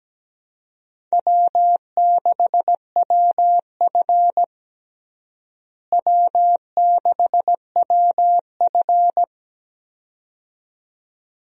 Morse code, heard as a keyed tone, about 17 wpm, 700 Hz. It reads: W6WF W6WF
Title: USA Call Signs - 4 Characters - Repeated 2x - 17wpm